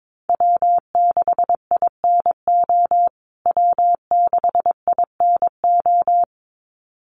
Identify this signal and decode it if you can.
Morse code, heard as a keyed tone, about 22 wpm, 700 Hz. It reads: W6INO W6INO
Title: USA Call Signs Repeated 2x 22wpm